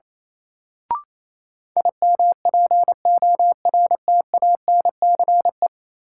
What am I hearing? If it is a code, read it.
Morse code, heard as a keyed tone, about 28 wpm, 700 Hz. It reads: IMPORTANCE
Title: Sentences from Top 1200 Words - New Word Drill - 28wpm